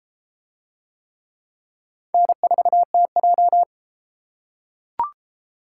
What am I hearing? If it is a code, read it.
Morse code, heard as a keyed tone, about 30 wpm, 700 Hz. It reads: N4TJ